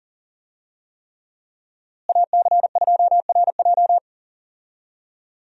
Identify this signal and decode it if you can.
Morse code, heard as a keyed tone, about 39 wpm, 700 Hz. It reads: AC2RJ
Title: USA Call Signs - 5 Characters - Repeated 3x - 40wpm